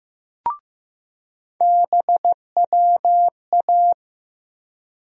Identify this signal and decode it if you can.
Morse code, heard as a keyed tone, about 15 wpm, 700 Hz. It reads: BWA
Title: USA Call Sign Postfixes - 3 Letters 15wpm